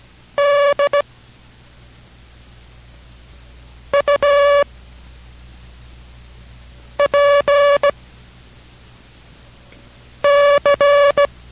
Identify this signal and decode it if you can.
Morse code, heard as a keyed tone, about 16 wpm, spaced as for 3 wpm, 570 Hz. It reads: DUPC